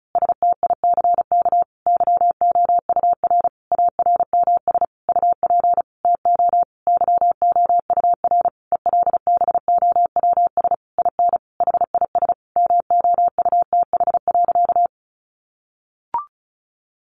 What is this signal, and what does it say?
Morse code, heard as a keyed tone, about 34 wpm, 700 Hz. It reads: STICK YOUR ARMS UP TO YOUR ELBOWS IN HIS MOUTH.
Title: A Sound Of Thunder 35wpm